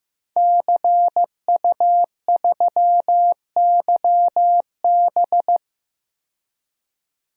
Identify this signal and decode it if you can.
Morse code, heard as a keyed tone, about 15 wpm, 700 Hz. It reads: CU3YB
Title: DXCC-100 Call Signs - 5 Characters - 15wpm